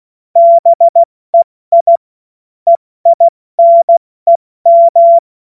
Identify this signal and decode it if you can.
Morse code, heard as a keyed tone, about 16 wpm, 680 Hz. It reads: BEI EINEM
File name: Morsekurs / Witze / Witz 30. Dezember 2025 70 F80